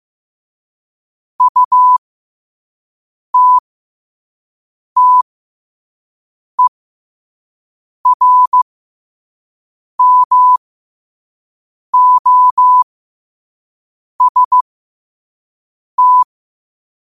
Morse code, heard as a keyed tone, about 15 wpm, spaced as for 5 wpm, 1000 Hz. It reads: UTTERMOST
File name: qotdfiles / 5wpm